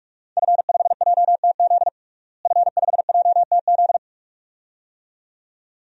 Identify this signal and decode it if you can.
Morse code, heard as a keyed tone, about 46 wpm, 700 Hz. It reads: F5JTZ F5JTZ
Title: DXCC-100 Call Signs - 5 Characters - Repeated 2x - 45wpm